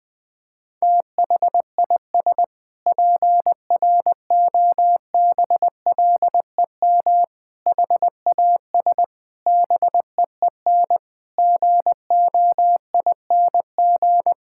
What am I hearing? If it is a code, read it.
Morse code, heard as a keyed tone, about 20 wpm, 700 Hz. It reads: THIS PROBLEM HAS BEEN GOING
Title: Sentences with 8-Word Phrases - 20wpm